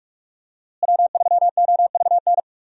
Morse code, heard as a keyed tone, about 46 wpm, 700 Hz. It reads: W3OVN
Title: USA Call Signs - 5 Characters - Repeated 3x - 45wpm